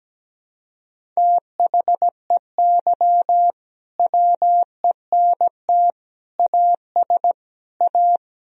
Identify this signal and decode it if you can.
Morse code, heard as a keyed tone, about 17 wpm, 700 Hz. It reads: THEY WENT AS A
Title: Sentences from Top 1000 Words - Review Part 1 - 17wpm